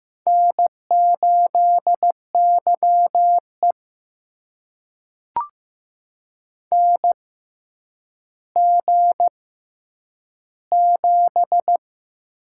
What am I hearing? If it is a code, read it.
Morse code, heard as a keyed tone, about 15 wpm, spaced as for 9 wpm, 700 Hz. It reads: N8YE NG7
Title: USA Call Signs - 4 Characters - 6x Character Spacing - 15wpm